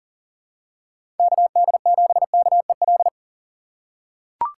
Morse code, heard as a keyed tone, about 40 wpm, 700 Hz. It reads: KD7KEL
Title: USA Call Signs - 6 Characters - 4x Character Spacing - 40wpm